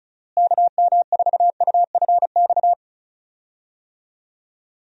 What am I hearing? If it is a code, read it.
Morse code, heard as a keyed tone, about 34 wpm, 700 Hz. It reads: KM4UFX
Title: USA Call Signs - 6 Characters - Repeated 2x - 35wpm